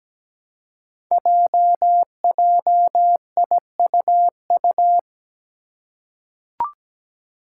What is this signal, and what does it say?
Morse code, heard as a keyed tone, about 17 wpm, 700 Hz. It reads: JJIUU
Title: Five Challenging Letter-Number Combinations 17wpm